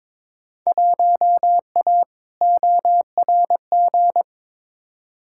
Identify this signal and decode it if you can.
Morse code, heard as a keyed tone, about 22 wpm, 700 Hz. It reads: 1A ORG
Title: Field Day Exchanges - 22wpm